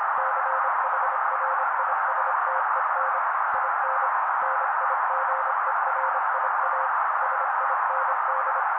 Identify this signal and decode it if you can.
Morse code, heard as a keyed tone, about 25 wpm, 530 Hz. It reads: CHRISTEN IN NIGERIA SIND
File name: daily / 25wpm noise